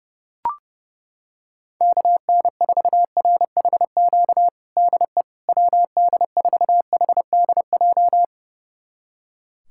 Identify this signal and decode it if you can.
Morse code, heard as a keyed tone, about 30 wpm, 700 Hz. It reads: KN4RHQ DE WD4HDJ
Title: USA Callsign DE Callsign - 6 Characters - 30wpm